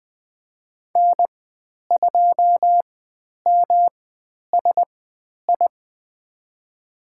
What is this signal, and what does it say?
Morse code, heard as a keyed tone, about 20 wpm, spaced as for 10 wpm, 700 Hz. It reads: N2MSI
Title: 109 Lesson 18g - Letter m - Call Signs 20wpm